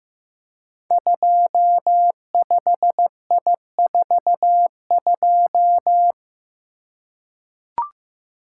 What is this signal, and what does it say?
Morse code, heard as a keyed tone, about 15 wpm, 700 Hz. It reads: 25I42
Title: Code Groups - 5 Letter:Numbers - 15wpm